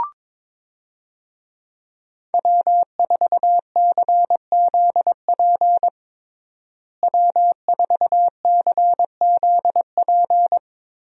Morse code, heard as a keyed tone, about 22 wpm, 700 Hz. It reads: W4CZP W4CZP
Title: USA Call Signs - 5 Characters - Repeated 3x - 22wpm